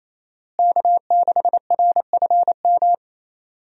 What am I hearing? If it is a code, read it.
Morse code, heard as a keyed tone, about 27 wpm, 700 Hz. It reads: K6RFM
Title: USA Call Signs - 5 Characters - Repeated 2x - 28wpm